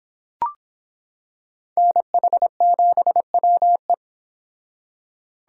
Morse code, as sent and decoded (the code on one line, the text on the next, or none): -. .... --... .-- .
NH7WE